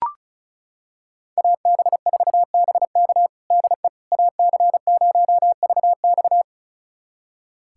Morse code, heard as a keyed tone, about 34 wpm, 700 Hz. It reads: AB4BK DE AC0VX
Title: USA Callsign DE Callsign - 5 Characters - 35wpm